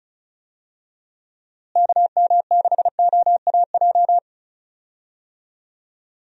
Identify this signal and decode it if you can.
Morse code, heard as a keyed tone, about 34 wpm, 700 Hz. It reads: KM6OAJ
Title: USA Call Signs - 6 Characters - Repeated 3x - 35wpm